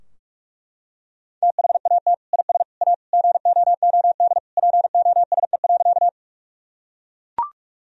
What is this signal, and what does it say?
Morse code, heard as a keyed tone, about 42 wpm, 700 Hz. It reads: THAT IS A GOOD POSE.
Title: Sentences from Top 1000 Words - Review Part 2 - 45wpm